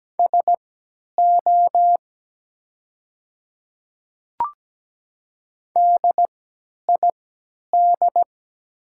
Morse code, heard as a keyed tone, about 17 wpm, spaced as for 10 wpm, 700 Hz. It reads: SO DID
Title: 3x Character Spacing - Top 300 Words - 17wpm